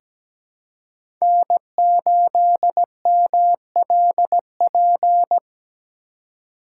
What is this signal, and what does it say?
Morse code, heard as a keyed tone, about 17 wpm, 700 Hz. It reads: N8MLP
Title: USA Call Signs - 5 Characters - 4x Character Spacing - 17wpm